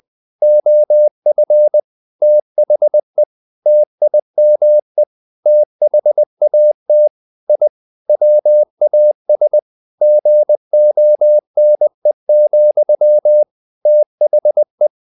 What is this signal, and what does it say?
Morse code, heard as a keyed tone, about 20 wpm, 600 Hz. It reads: OF THE TIME THAT I WAS GONE, THE
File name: the old man and the sea part 2 20wpm